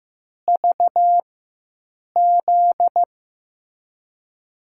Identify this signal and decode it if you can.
Morse code, heard as a keyed tone, about 15 wpm, spaced as for 7 wpm, 700 Hz. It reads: VZ
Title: USA Call Signs - 5 Characters - 4x Character Spacing - 15wpm